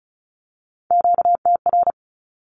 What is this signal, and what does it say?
Morse code, heard as a keyed tone, about 36 wpm, 700 Hz. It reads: QTR
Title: Q-Codes with Spoken Meaning - 35wpm